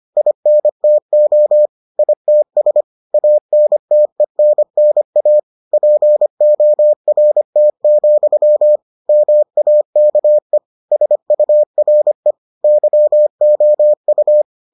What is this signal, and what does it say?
Morse code, heard as a keyed tone, about 25 wpm, 600 Hz. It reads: INTO ITS ANTENNA PORT, MAKE SURE YOU